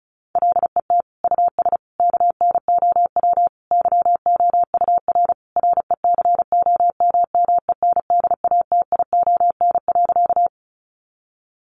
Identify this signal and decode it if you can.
Morse code, heard as a keyed tone, about 34 wpm, 700 Hz. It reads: LET US KNOW YOUR RECOMMENDATION.
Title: Sentences from Top 1400 Words 35wpm